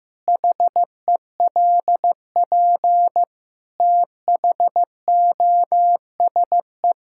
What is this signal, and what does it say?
Morse code, heard as a keyed tone, about 15 wpm, 700 Hz. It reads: HELP THOSE
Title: Sentences from Top 1000 Words - Review Part 5 - 15wpm